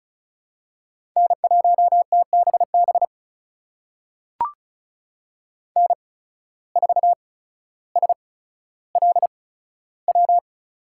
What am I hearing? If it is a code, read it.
Morse code, heard as a keyed tone, about 34 wpm, spaced as for 15 wpm, 700 Hz. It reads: N1TBB N4SLW
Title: USA Call Signs - 5 Characters - 8x Character Spacing - 35wpm